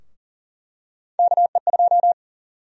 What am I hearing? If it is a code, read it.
Morse code, heard as a keyed tone, about 40 wpm, 700 Hz. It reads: KE2
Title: USA Call Sign Prefixes - 40wpm